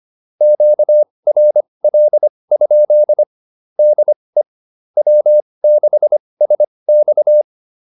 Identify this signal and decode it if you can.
Morse code, heard as a keyed tone, about 25 wpm, spaced as for 20 wpm, 600 Hz. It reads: QRL? DE W6SX
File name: QSO102 20